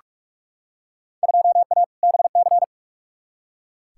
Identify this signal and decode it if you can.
Morse code, heard as a keyed tone, about 42 wpm, 700 Hz. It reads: 2A BC
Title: Field Day Exchanges - 45wpm